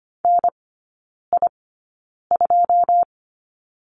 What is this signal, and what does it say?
Morse code, heard as a keyed tone, about 25 wpm, spaced as for 9 wpm, 690 Hz. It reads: NI2
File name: Lesson 14e - Number 2 - Three Characters 25wpm